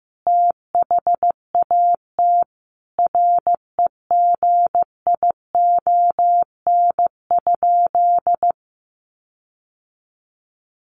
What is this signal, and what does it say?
Morse code, heard as a keyed tone, about 15 wpm, 700 Hz. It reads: THAT REGION?
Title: Sentences from Top 1500 Words 15wpm